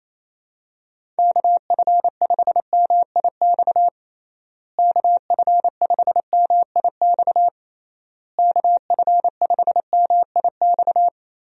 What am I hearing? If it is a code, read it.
Morse code, heard as a keyed tone, about 28 wpm, 700 Hz. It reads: KF5MIX KF5MIX KF5MIX
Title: USA Call Signs - 6 Characters - Repeated 3x - 28wpm